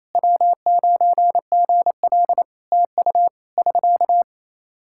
Morse code, heard as a keyed tone, about 28 wpm, 700 Hz. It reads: W9GL TU <SK>
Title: K1USN SST QSO Practice 28wpm